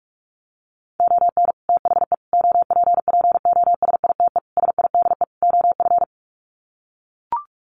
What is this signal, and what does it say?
Morse code, heard as a keyed tone, about 44 wpm, 700 Hz. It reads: ON THE OPPOSITE SIDE OF
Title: Sets of 5 Words Encore2 45wpm